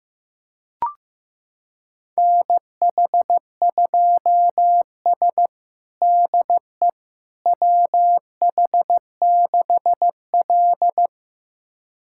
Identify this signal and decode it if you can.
Morse code, heard as a keyed tone, about 16 wpm, 700 Hz. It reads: NH2S DE WH6L
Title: USA Callsign DE Callsign - 4 Characters - 15wpm